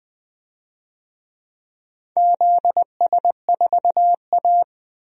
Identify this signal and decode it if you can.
Morse code, heard as a keyed tone, about 20 wpm, 700 Hz. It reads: ZS4A
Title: DXCC-100 Call Signs - 4 Characters - 20wpm